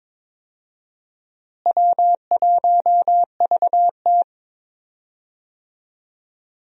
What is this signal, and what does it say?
Morse code, heard as a keyed tone, about 21 wpm, 700 Hz. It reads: W1VT
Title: USA Call Signs - 4 Characters - Repeated 3x - 22wpm